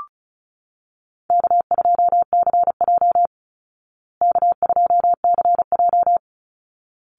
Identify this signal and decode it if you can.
Morse code, heard as a keyed tone, about 34 wpm, 700 Hz. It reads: K2CJ K2CJ
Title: Easy Call Signs Repeated 2x - 35wpm